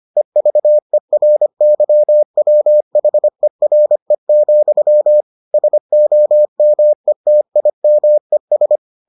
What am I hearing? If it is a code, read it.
Morse code, heard as a keyed tone, about 25 wpm, 600 Hz. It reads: EVERYWHERE, SOMETIMES